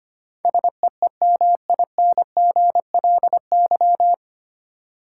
Morse code, heard as a keyed tone, about 25 wpm, 700 Hz. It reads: SEEMINGLY
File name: Nine Letter Words - 25wpm